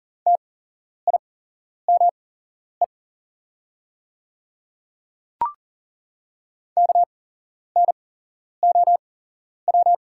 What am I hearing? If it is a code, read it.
Morse code, heard as a keyed tone, about 40 wpm, spaced as for 11 wpm, 700 Hz. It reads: TIME KNOW
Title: 8x Character Spacing - Top 100 Words - 40wpm